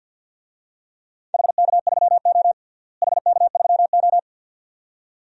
Morse code, heard as a keyed tone, about 52 wpm, 700 Hz. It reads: HK3O HK3O
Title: DXCC-100 Call Signs - 4 Characters - Repeated 2x - 50wpm